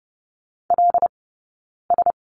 Morse code, heard as a keyed tone, about 30 wpm, 700 Hz. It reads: L S
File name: Lesson 19d - Letter w - Two Characters 30wpm